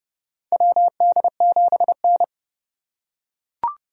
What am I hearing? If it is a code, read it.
Morse code, heard as a keyed tone, about 32 wpm, 700 Hz. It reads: WD7N
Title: USA Call Signs - 4 Characters - 4x Character Spacing - 30wpm